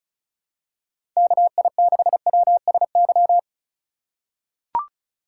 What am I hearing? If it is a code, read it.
Morse code, heard as a keyed tone, about 34 wpm, 700 Hz. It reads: KI6WSY
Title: USA Call Signs - 6 Characters - 3x Character Spacing - 35wpm